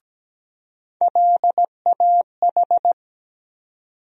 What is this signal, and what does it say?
Morse code, heard as a keyed tone, about 17 wpm, 700 Hz. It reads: LAH
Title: USA Call Sign Postfixes - 3 Letters 17wpm